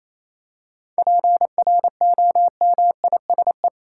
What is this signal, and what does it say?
Morse code, heard as a keyed tone, about 28 wpm, 700 Hz. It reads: PROMISE
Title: Seven Letter Words - 28wpm